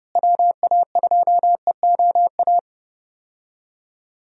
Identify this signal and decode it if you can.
Morse code, heard as a keyed tone, about 30 wpm, 700 Hz. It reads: WA2EOA
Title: USA Call Signs - 6 Characters - Repeated 3x - 30wpm